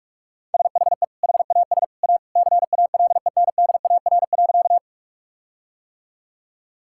Morse code, heard as a keyed tone, about 46 wpm, 700 Hz. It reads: SHE HAS A CALENDAR.
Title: Sentences from Top 1600 Words 45wpm